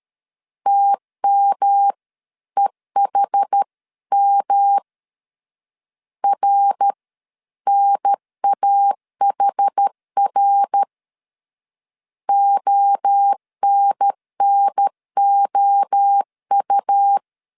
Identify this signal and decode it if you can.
Morse code, heard as a keyed tone, about 13 wpm, spaced as for 11 wpm, 800 Hz. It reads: TMEHM RNAHR ONNOU
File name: CW / DET / 1 3 1500C S B 12wpm 2828